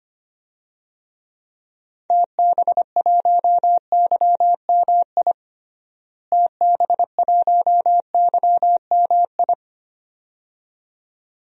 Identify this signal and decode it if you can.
Morse code, heard as a keyed tone, about 25 wpm, 700 Hz. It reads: TB1YMI TB1YMI
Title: DXCC-100 Call Signs - 6 Characters - Repeated 2x - 25wpm